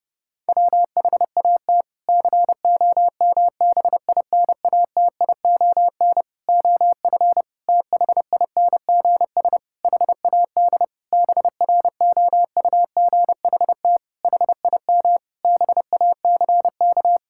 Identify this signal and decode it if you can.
Morse code, heard as a keyed tone, about 30 wpm, 700 Hz. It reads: WHAT COMBINATION OF THINGS HAD BROUGHT HIM BACK
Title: Sentences from Top 1000 Words - Review Part 6 - 30wpm